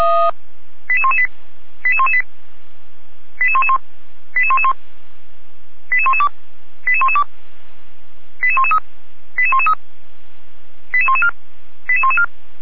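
Five-tone selective call (ZVEI), heard as quick runs of five tones, tones of about 70 ms. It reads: 88118 88118 88111 88111 88112 88112 88113 88113 88114 88114
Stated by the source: ZVEI five-tone call received on an SDR.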